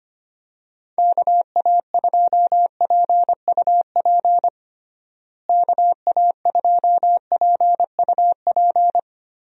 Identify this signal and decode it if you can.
Morse code, heard as a keyed tone, about 25 wpm, 700 Hz. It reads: KA2PUP KA2PUP
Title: USA Call Signs - 6 Characters - Repeated 3x - 25wpm